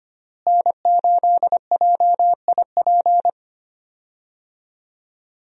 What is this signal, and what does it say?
Morse code, heard as a keyed tone, about 25 wpm, 700 Hz. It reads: N8JIP
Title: USA Call Signs - 5 Characters - Repeated 2x - 25wpm